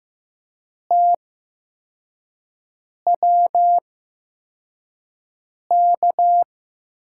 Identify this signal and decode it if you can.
Morse code, heard as a keyed tone, about 15 wpm, spaced as for 4 wpm, 700 Hz. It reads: TWK
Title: USA Call Signs - 6 Characters - 8x Character Spacing - 15wpm